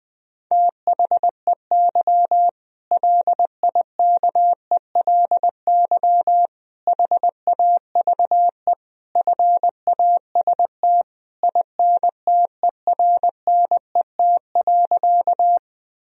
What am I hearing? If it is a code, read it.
Morse code, heard as a keyed tone, about 20 wpm, 700 Hz. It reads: THEY LIKELY HAVE FAST INTERNET.
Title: Sentences from Top 1000 Words - Review Part 4 - 20wpm